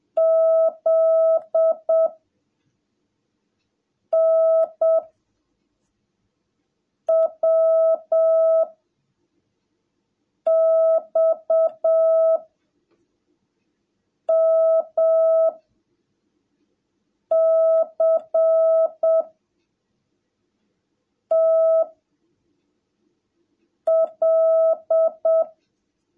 Morse code, heard as a keyed tone, about 7 wpm, 660 Hz. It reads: Z N W X M C T L
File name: jh8cbh / 2021cw / cw17a